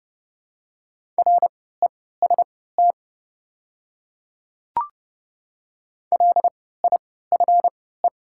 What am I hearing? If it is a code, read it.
Morse code, heard as a keyed tone, about 30 wpm, spaced as for 17 wpm, 700 Hz. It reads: REST LIFE
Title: 3x Character Spacing - Top 500 Words - 30wpm